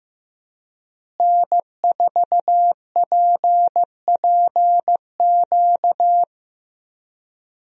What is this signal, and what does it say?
Morse code, heard as a keyed tone, about 16 wpm, 700 Hz. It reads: N4PPQ
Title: USA Call Signs - 5 Characters - Repeated 3x - 15wpm